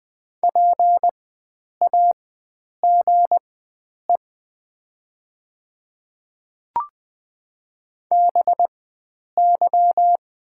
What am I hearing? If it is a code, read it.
Morse code, heard as a keyed tone, about 20 wpm, spaced as for 9 wpm, 700 Hz. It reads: PAGE BY
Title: 4x Character Spacing - Top 300 Words - 20wpm